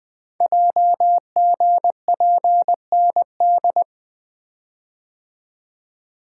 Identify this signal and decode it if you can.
Morse code, heard as a keyed tone, about 20 wpm, 700 Hz. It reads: JGPND
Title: Code Groups - 5 Letters - 20wpm